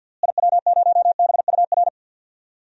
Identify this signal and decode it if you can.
Morse code, heard as a keyed tone, about 49 wpm, 700 Hz. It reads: IW0BUR
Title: DXCC-100 Call Signs - 6 Characters - Repeated 3x - 50wpm